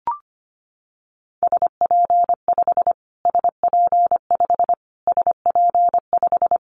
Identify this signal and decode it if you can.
Morse code, heard as a keyed tone, about 25 wpm, 700 Hz. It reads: SP5 SP5 SP5
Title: Callsign Prefixes for Top 25 DXCC - Repeated 3x - 25wpm